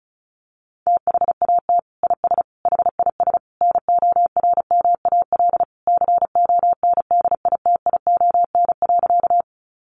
Morse code, heard as a keyed tone, about 34 wpm, 700 Hz. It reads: THAT IS HIS NORMAL CONDITION.
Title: Sentences from Top 1200 Words 35wpm